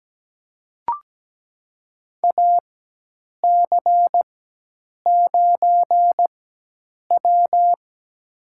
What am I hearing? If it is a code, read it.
Morse code, heard as a keyed tone, about 17 wpm, spaced as for 8 wpm, 700 Hz. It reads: AC9W
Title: USA Call Signs - 4 Characters - 4x Character Spacing - 17wpm